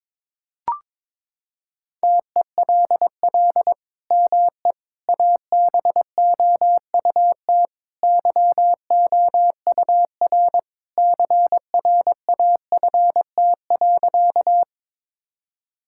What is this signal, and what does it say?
Morse code, heard as a keyed tone, about 22 wpm, 700 Hz. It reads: TELL ME ABOUT YOUR CRAFT.
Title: Sentences from Top 1000 Words - Review Part 6 - 22wpm